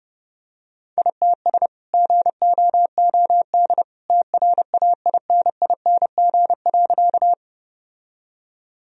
Morse code, heard as a keyed tone, about 32 wpm, 700 Hz. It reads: ITS GOOD TRAINING.
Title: Sentences from Top 1000 Words - Review Part 4 - 30wpm